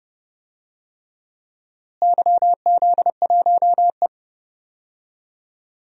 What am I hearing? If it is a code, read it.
Morse code, heard as a keyed tone, about 30 wpm, 700 Hz. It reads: YZ1E
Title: DXCC-100 Call Signs - 4 Characters - Repeated 3x - 30wpm